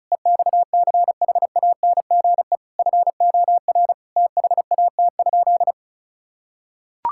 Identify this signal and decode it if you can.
Morse code, heard as a keyed tone, about 34 wpm, 700 Hz. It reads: EXCHANGE FOR THAT?
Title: Sentences from Top 1000 Words - Review Part 6 - 35wpm